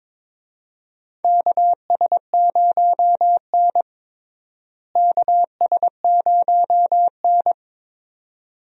Morse code, heard as a keyed tone, about 22 wpm, 700 Hz. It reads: KS0N KS0N
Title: USA Call Signs - 4 Characters - Repeated 2x - 22wpm